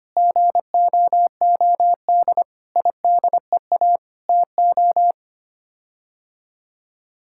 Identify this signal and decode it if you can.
Morse code, heard as a keyed tone, about 25 wpm, 700 Hz. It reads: GOOD IDEA TO
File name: Sets of 9 Words - 25wpm